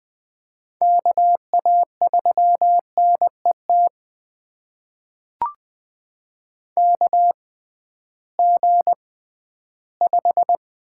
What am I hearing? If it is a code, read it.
Morse code, heard as a keyed tone, about 20 wpm, spaced as for 13 wpm, 700 Hz. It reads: KA3NET KG5
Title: USA Call Signs - 6 Characters - 6x Character Spacing - 20wpm